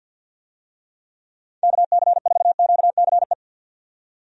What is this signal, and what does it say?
Morse code, heard as a keyed tone, about 47 wpm, 700 Hz. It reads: KC4QCE
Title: USA Call Signs - 6 Characters - 50wpm